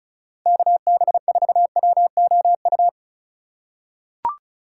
Morse code, heard as a keyed tone, about 34 wpm, 700 Hz. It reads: KB4WOU
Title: USA Call Signs - 6 Characters - 2x Character Spacing - 35wpm